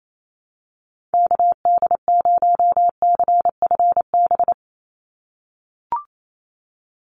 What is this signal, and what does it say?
Morse code, heard as a keyed tone, about 28 wpm, 700 Hz. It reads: KD0CFB
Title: USA Call Signs 28wpm